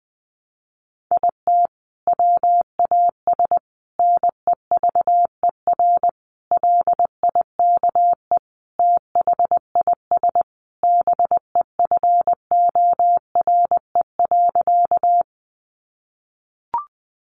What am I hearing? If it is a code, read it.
Morse code, heard as a keyed tone, about 20 wpm, 700 Hz. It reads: IT WAS NEVER LIKE THIS BEFORE.
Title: A Sound Of Thunder 20wpm